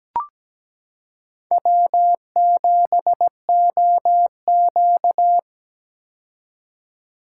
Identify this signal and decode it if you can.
Morse code, heard as a keyed tone, about 17 wpm, 700 Hz. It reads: W7OQ
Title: K1USN SST QSO Practice - 17wpm